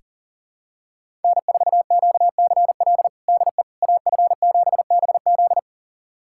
Morse code, heard as a keyed tone, about 37 wpm, 700 Hz. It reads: N4QCL DE AF7BZ
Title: USA Callsign DE Callsign - 5 Characters - 40wpm